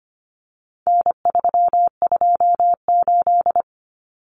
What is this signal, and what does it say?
Morse code, heard as a keyed tone, about 25 wpm, 700 Hz. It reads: N328
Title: Common Cut Numbers - 4-Digit 25wpm